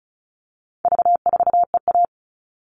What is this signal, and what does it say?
Morse code, heard as a keyed tone, about 34 wpm, 700 Hz. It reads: V4EA
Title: All Cut Numbers - 4-Digit 35wpm